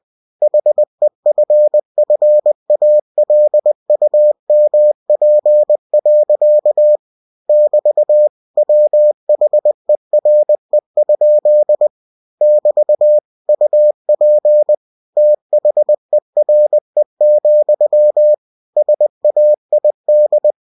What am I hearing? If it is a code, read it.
Morse code, heard as a keyed tone, about 20 wpm, 600 Hz. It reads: HEFFALUMP. <BT> WHERE? <BT> UP THERE, SAID